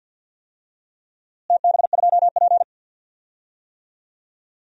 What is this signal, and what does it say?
Morse code, heard as a keyed tone, about 48 wpm, 700 Hz. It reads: TB2P
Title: DXCC-100 Call Signs - 4 Characters - Repeated 2x - 50wpm